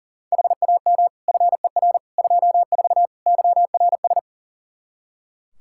Morse code, heard as a keyed tone, about 40 wpm, 700 Hz. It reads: HAM FER 24 YRS
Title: Practice QSO - Exchange 3 - More Info - 40wpm